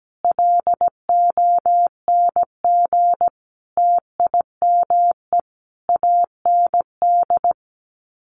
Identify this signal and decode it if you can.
Morse code, heard as a keyed tone, about 17 wpm, 700 Hz. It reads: LONG TIME AND
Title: Sets of 5 Words Encore2 17wpm